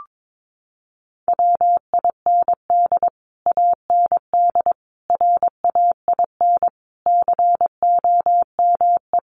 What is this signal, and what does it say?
Morse code, heard as a keyed tone, about 22 wpm, 700 Hz. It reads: WIND AND RAIN COME
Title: Sentences from Top 1400 Words 22wpm